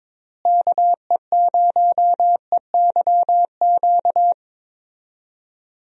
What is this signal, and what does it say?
Morse code, heard as a keyed tone, about 21 wpm, 700 Hz. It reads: KE0EYQ
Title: USA Call Signs - 6 Characters - Repeated 2x - 22wpm